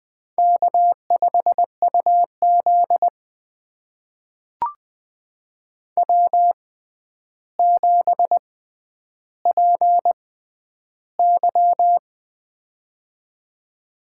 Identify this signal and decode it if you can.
Morse code, heard as a keyed tone, about 20 wpm, spaced as for 10 wpm, 700 Hz. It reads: K5UZ W7PY
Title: USA Call Signs - 4 Characters - 6x Character Spacing - 20wpm